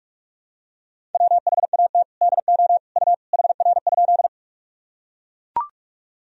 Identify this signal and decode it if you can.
Morse code, heard as a keyed tone, about 42 wpm, 700 Hz. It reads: WHAT DO U HR?
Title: Sentences from Top 300 Words and Abbreviations 45wpm